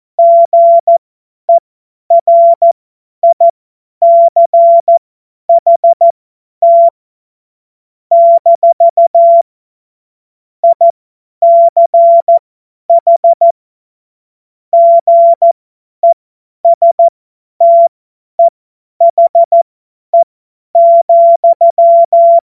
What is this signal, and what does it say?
Morse code, heard as a keyed tone, about 14 wpm, spaced as for 10 wpm, 680 Hz. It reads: GERICHT - ICH GESTEHE,
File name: Morsekurs / Witze / Witz 18. Mai 2021, 50 F70